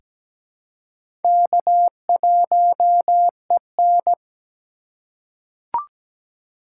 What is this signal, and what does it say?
Morse code, heard as a keyed tone, about 17 wpm, 700 Hz. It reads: K1EN
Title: USA Call Signs - 4 Characters - 2x Character Spacing - 17wpm